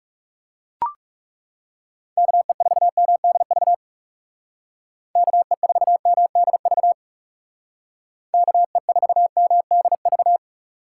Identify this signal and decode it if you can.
Morse code, heard as a keyed tone, about 38 wpm, 700 Hz. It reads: KE4MDV KE4MDV KE4MDV
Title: Speed Racing - USA 2x3 Call Signs 30wpm